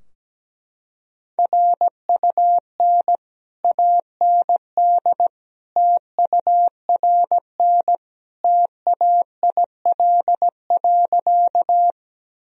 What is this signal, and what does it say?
Morse code, heard as a keyed tone, about 17 wpm, 700 Hz. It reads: RUN AND TURN TAIL.
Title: Sentences from Top 1000 Words - Review Part 3 - 17wpm